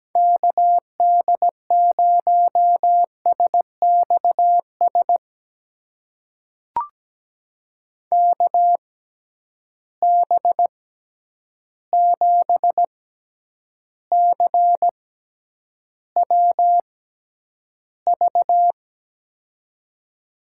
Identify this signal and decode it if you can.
Morse code, heard as a keyed tone, about 17 wpm, spaced as for 9 wpm, 700 Hz. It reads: KD0SXS KB7CWV
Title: USA Call Signs - 6 Characters - 6x Character Spacing - 17wpm